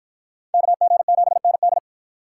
Morse code, heard as a keyed tone, about 51 wpm, 700 Hz. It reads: KG7ND